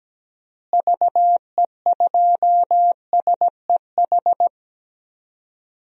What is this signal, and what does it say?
Morse code, heard as a keyed tone, about 17 wpm, 700 Hz. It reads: VE2SEH
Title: DXCC-100 Call Signs - 6 Characters - Repeated 2x - 17wpm